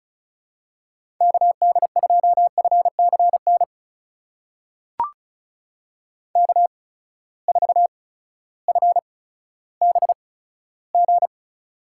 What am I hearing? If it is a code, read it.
Morse code, heard as a keyed tone, about 36 wpm, spaced as for 16 wpm, 700 Hz. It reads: KD2FCN K4FBG
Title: USA Call Signs - 8x Character Spacing - 35wpm